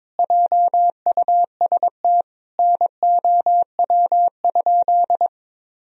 Morse code, heard as a keyed tone, about 21 wpm, 700 Hz. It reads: JUST NOW?
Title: Sentences from Top 1000 Words - Review Part 3 - 22wpm